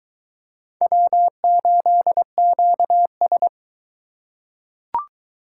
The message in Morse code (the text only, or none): W8QS